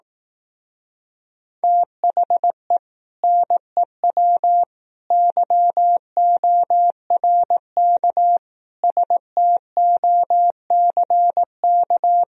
Morse code, heard as a keyed tone, about 18 wpm, 700 Hz. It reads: THE NEW YORK STOCK
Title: Speed-Racing - Sets of 5 Words 15wpm